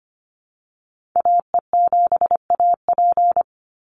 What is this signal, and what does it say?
Morse code, heard as a keyed tone, about 25 wpm, 690 Hz. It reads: AE7AP
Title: NA SOTA Activator Callsigns 25wpm